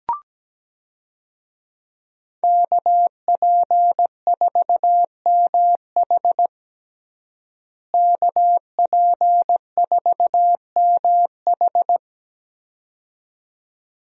Morse code, heard as a keyed tone, about 17 wpm, 700 Hz. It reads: KP4MH KP4MH
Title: DXCC-100 Call Signs - 5 Characters - Repeated 2x - 17wpm